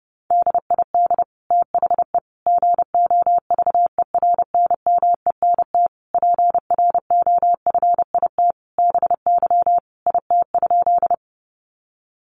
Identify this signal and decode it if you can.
Morse code, heard as a keyed tone, about 30 wpm, 700 Hz. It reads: DID THE GOVERNMENT PROFIT BY IT?
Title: Sentences from Top 800 Words 30wpm